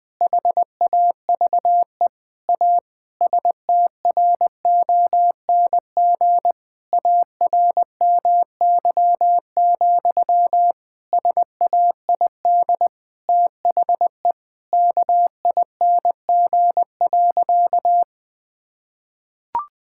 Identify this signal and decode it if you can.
Morse code, heard as a keyed tone, about 20 wpm, 700 Hz. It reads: HAVE A STRONG ARMY, SAID THE KING.
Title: Sentences from Top 1000 Words - Review Part 5 - 20wpm